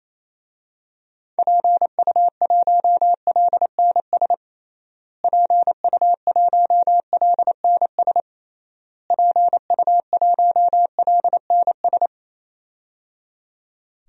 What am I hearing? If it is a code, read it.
Morse code, heard as a keyed tone, about 28 wpm, 700 Hz. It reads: PU1LNS PU1LNS PU1LNS
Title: DXCC-100 Call Signs - 6 Characters - Repeated 3x - 28wpm